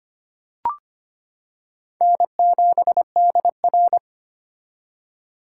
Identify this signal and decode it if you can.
Morse code, heard as a keyed tone, about 25 wpm, 700 Hz. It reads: N7DR
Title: K1USN SST QSO Practice - 25wpm